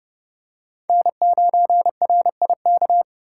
Morse code, heard as a keyed tone, about 32 wpm, 700 Hz. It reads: N9RIK
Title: USA Call Signs - 5 Characters - 3x Character Spacing - 30wpm